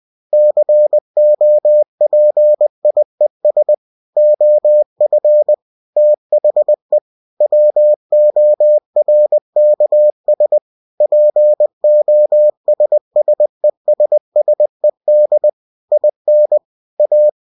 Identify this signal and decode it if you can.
Morse code, heard as a keyed tone, about 20 wpm, 600 Hz. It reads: COPIES OF THE WORKS POSSESSED IN A
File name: cw-books / winnie-the-pooh / winnie-20wpm-page-0117